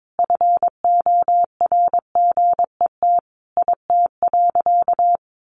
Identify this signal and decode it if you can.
Morse code, heard as a keyed tone, about 22 wpm, 700 Hz. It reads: FORGET IT.
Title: Sentences from Top 1200 Words 22wpm